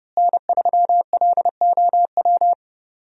Morse code, heard as a keyed tone, about 30 wpm, 700 Hz. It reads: N3LOW
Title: USA Call Signs - 5 Characters - Repeated 3x - 30wpm